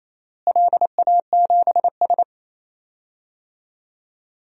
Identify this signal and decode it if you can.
Morse code, heard as a keyed tone, about 28 wpm, 700 Hz. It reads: LA7S
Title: DXCC-100 Call Signs - 4 Characters - Repeated 3x - 28wpm